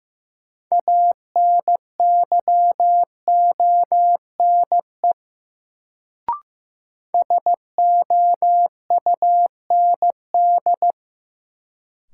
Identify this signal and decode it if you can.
Morse code, heard as a keyed tone, about 15 wpm, 700 Hz. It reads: ANYONE SOUND
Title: Sentences from Top 900 Words - New Word Drill - 15wpm